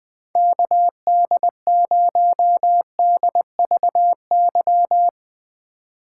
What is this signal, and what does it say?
Morse code, heard as a keyed tone, about 20 wpm, 700 Hz. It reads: KD0DVY
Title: Speed Racing - USA 2x3 Call Signs 17wpm